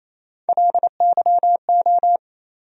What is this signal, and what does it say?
Morse code, heard as a keyed tone, about 28 wpm, 700 Hz. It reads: LYO
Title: USA Call Sign Postfixes - 3 Letters 28wpm